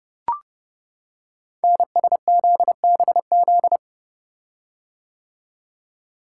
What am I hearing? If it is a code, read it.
Morse code, heard as a keyed tone, about 30 wpm, 690 Hz. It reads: NSZBZ
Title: Code Groups - 5 Letters - 30wpm